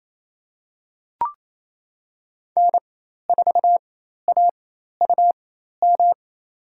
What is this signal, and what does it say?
Morse code, heard as a keyed tone, about 28 wpm, spaced as for 13 wpm, 700 Hz. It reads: N4AUM
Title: USA Call Signs - 4x Character Spacing - 28wpm